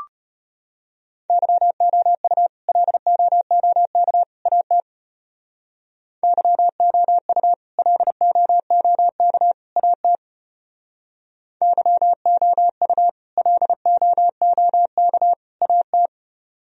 Morse code, heard as a keyed tone, about 33 wpm, 700 Hz. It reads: YOU LOOK AT YOU LOOK AT YOU LOOK AT